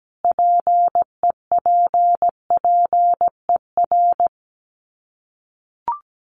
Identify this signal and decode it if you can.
Morse code, heard as a keyed tone, about 17 wpm, 690 Hz. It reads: PEPPER
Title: Sets of 5 Words Encore 17wpm